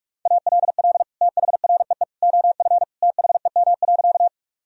Morse code, heard as a keyed tone, about 41 wpm, 700 Hz. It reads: ALL THREE OF THEM.
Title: Sentences from Top 1000 Words - Review Part 1 - 45wpm